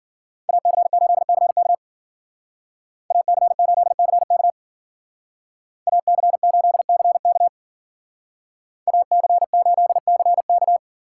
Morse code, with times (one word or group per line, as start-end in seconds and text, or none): 0.5-1.7 AC8CK
3.1-4.5 AC8CK
5.9-7.5 AC8CK
8.9-10.8 AC8CK